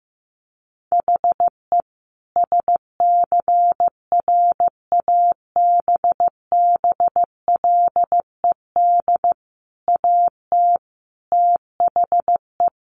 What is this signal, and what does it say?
Morse code, heard as a keyed tone, about 15 wpm, 700 Hz. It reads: HE SCRABBLED AT THE
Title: A Sound Of Thunder 15wpm